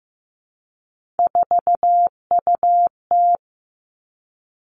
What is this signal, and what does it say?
Morse code, heard as a keyed tone, about 15 wpm, 700 Hz. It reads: 4UT
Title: Three Challenging Letter-Number Combinations 15wpm